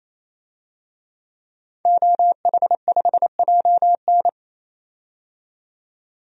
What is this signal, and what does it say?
Morse code, heard as a keyed tone, about 28 wpm, 700 Hz. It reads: OH5JN
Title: DXCC-100 Call Signs - 5 Characters - 28wpm